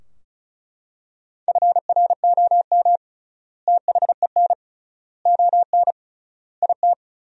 Morse code, heard as a keyed tone, about 34 wpm, 700 Hz. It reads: FROM THEN ON IT
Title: 3x Word-Spacing - Sentences from Top 400 Words - 35wpm